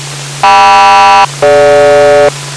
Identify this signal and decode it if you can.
Aircraft SELCAL, heard as two pairs of tones. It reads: KM-DG